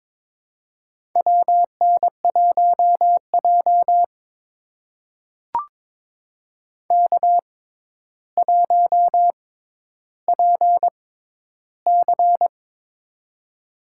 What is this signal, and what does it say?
Morse code, heard as a keyed tone, about 22 wpm, spaced as for 11 wpm, 700 Hz. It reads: WN1J K1PC
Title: USA Call Signs - 6x Character Spacing - 22wpm